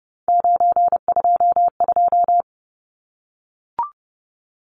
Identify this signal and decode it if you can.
Morse code, heard as a keyed tone, about 30 wpm, 700 Hz. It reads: 922
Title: Numbers - 3-Digit 30wpm